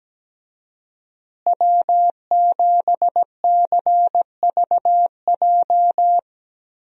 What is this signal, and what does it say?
Morse code, heard as a keyed tone, about 17 wpm, 700 Hz. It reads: W7CVJ
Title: USA Call Signs - 5 Characters - Repeated 2x - 17wpm